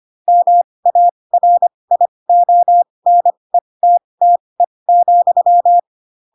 Morse code, heard as a keyed tone, about 25 wpm, spaced as for 20 wpm, 700 Hz. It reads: MARIONETTE,
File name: Pinocchio / Pinocchio ch8 15 25 ik1hge